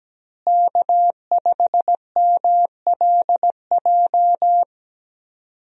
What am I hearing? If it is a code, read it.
Morse code, heard as a keyed tone, about 17 wpm, 700 Hz. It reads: K5MLJ
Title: USA Call Signs - 5 Characters - Repeated 2x - 17wpm